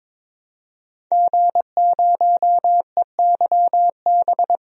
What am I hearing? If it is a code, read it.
Morse code, heard as a keyed tone, about 22 wpm, 700 Hz. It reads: G0EYB
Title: DXCC-100 Call Signs - 5 Characters - 22wpm